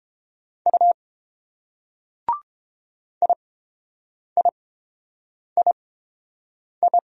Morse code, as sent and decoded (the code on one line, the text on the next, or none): ..- / .. .. .. ..
U IIII